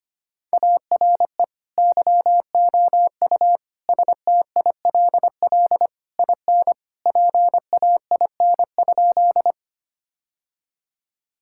Morse code, heard as a keyed tone, about 25 wpm, 700 Hz. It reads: ARE YOU STILL IN PAIN?
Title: Sentences from Top 1600 Words 25wpm